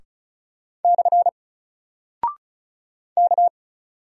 Morse code, heard as a keyed tone, about 36 wpm, 700 Hz. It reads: / K
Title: Single Character (Letter:Number:Punctuation:Prosign) 35wpm